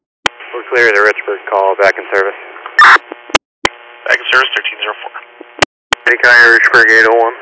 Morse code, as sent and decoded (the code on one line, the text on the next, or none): none